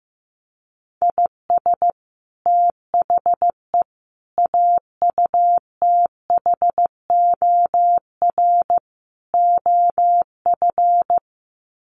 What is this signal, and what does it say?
Morse code, heard as a keyed tone, about 15 wpm, 700 Hz. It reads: IS THE AUTHOR OF
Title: Sets of 4 Words 15wpm